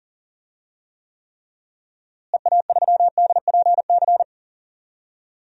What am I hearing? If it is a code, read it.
Morse code, heard as a keyed tone, about 40 wpm, 700 Hz. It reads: EA3DPC
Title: DXCC-100 Call Signs - 6 Characters - Repeated 3x - 40wpm